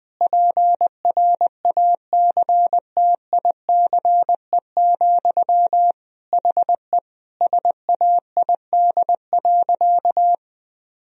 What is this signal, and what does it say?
Morse code, heard as a keyed tone, about 20 wpm, 700 Hz. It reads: PRACTICE, HE SAID.
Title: Sentences from Top 1000 Words - Review Part 4 - 20wpm